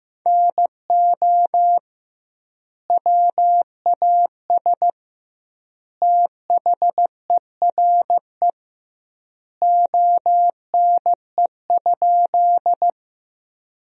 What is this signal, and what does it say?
Morse code, heard as a keyed tone, about 15 wpm, 700 Hz. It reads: NO WAS THERE ONE?
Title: 2x Word-Spacing - Sentences from Top 100 Words - 15wpm